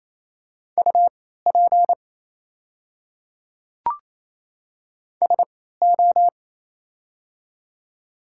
Morse code, heard as a keyed tone, about 27 wpm, spaced as for 16 wpm, 700 Hz. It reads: UP SO
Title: 3x Character Spacing - Top 100 Words - 28wpm